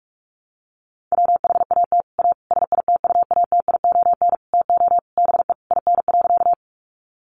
Morse code, heard as a keyed tone, about 44 wpm, 700 Hz. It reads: WHAT A SITUATION TO BE IN.
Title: Sentences from Top 1200 Words 45wpm